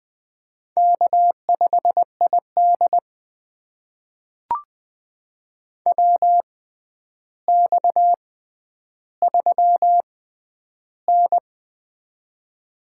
Morse code, heard as a keyed tone, about 20 wpm, spaced as for 10 wpm, 700 Hz. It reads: K5ID WX3N
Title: USA Call Signs - 4 Characters - 6x Character Spacing - 20wpm